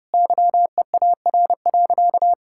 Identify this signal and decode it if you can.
Morse code, heard as a keyed tone, about 32 wpm, 700 Hz. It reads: YEAR.
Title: Sentences from Top 1600 Words 30wpm